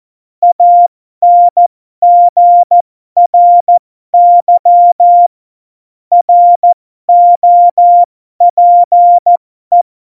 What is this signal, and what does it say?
Morse code, heard as a keyed tone, about 14 wpm, spaced as for 12 wpm, 700 Hz. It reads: ANGRY ROPE
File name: Morsekurs / Witze